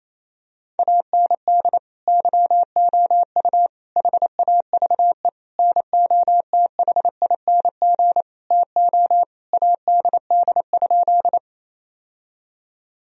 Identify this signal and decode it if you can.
Morse code, heard as a keyed tone, about 28 wpm, 700 Hz. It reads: AND YOU HAVE NOTHING TO ADD?
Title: Sentences from Top 1000 Words - Review Part 1 - 28wpm